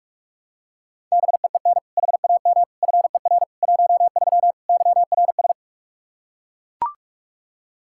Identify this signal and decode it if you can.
Morse code, heard as a keyed tone, about 41 wpm, 700 Hz. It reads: BEEN HAM FER 13 YRS
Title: Practice QSO - Exchange 3 - More Info - 45wpm